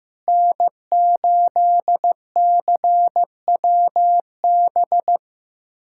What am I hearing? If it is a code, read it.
Morse code, heard as a keyed tone, about 15 wpm, 700 Hz. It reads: N8CWB
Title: USA Callsign DE Callsign - 5 Characters - 15wpm